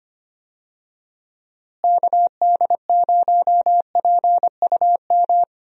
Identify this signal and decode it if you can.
Morse code, heard as a keyed tone, about 25 wpm, 700 Hz. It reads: KD0PUM
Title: USA Call Signs - 6 Characters - Repeated 3x - 25wpm